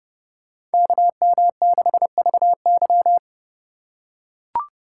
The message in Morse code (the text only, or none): KM6VY